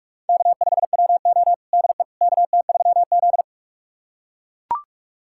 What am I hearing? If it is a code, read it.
Morse code, heard as a keyed tone, about 42 wpm, 700 Hz. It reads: K5WO DE KT3Z
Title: USA Callsign DE Callsign - 4 Characters - 45wpm